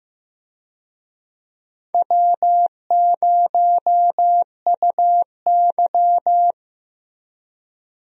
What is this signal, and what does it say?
Morse code, heard as a keyed tone, about 15 wpm, 700 Hz. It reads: W0UY
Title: USA Call Signs - 4 Characters - Repeated 3x - 15wpm